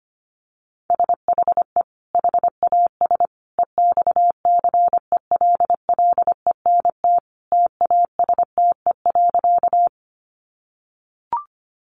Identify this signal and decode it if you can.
Morse code, heard as a keyed tone, about 25 wpm, 700 Hz. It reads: SHE HAS EXCELLENT TASTE.
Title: Sentences from Top 1100 Words 25wpm